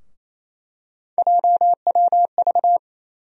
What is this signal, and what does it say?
Morse code, heard as a keyed tone, about 28 wpm, 700 Hz. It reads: JWV
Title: USA Call Sign Postfixes - 3 Letters 28wpm